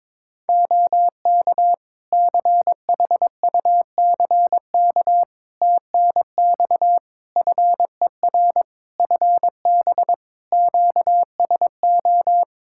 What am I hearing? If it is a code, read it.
Morse code, heard as a keyed tone, about 22 wpm, 700 Hz. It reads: OK CHUCK TNX FER FB QSO
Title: Practice QSO - Exchange 4 - Ending - 22wpm